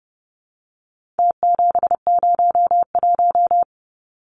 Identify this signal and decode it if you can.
Morse code, heard as a keyed tone, about 30 wpm, 700 Hz. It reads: T701
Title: Numbers - Contest Style - 4-Digit 30wpm